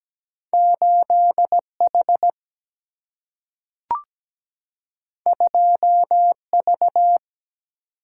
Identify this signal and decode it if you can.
Morse code, heard as a keyed tone, about 17 wpm, 700 Hz. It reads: 8H 2V
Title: Single-Letter Single-Number 17wpm